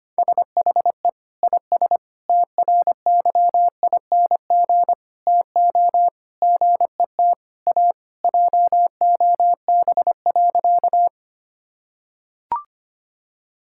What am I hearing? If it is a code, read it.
Morse code, heard as a keyed tone, about 25 wpm, 700 Hz. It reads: SHE IS TRYING TO GET A JOB.
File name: practice / 25wpm / Sentences with 3-Word Phrases - 25wpm